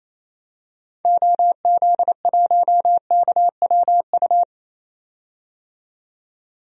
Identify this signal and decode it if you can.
Morse code, heard as a keyed tone, about 28 wpm, 700 Hz. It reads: OZ1KWU
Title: DXCC-100 Call Signs - 6 Characters - 28wpm